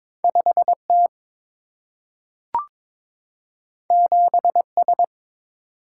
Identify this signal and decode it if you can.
Morse code, heard as a keyed tone, about 22 wpm, 700 Hz. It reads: 5T 7S
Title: Single-Letter Single-Number 22wpm